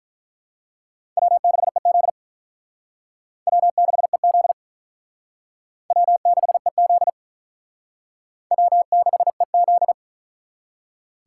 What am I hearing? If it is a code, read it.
Morse code, heard as a keyed tone, about 42 wpm, 700 Hz. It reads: W6EZ W6EZ W6EZ W6EZ
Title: Speed Racing - USA 1x2 Call Signs 35wpm